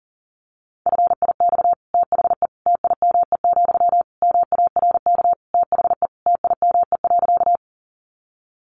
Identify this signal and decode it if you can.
Morse code, heard as a keyed tone, about 40 wpm, 700 Hz. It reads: FIX THE TIME, MARK THE TIME.
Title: Sentences from Top 1400 Words 40wpm